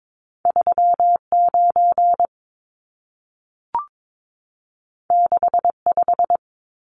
Morse code, heard as a keyed tone, about 22 wpm, 700 Hz. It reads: 39 65
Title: Numbers - 2-Digit 22wpm